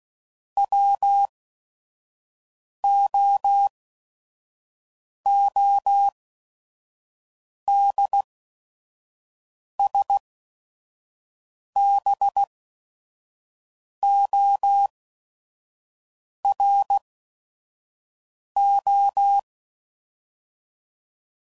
Morse code, heard as a keyed tone, about 16 wpm, spaced as for 5 wpm, 800 Hz. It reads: WOODSBORO
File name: CW Code Groups @5WPM / fromSM's / Cities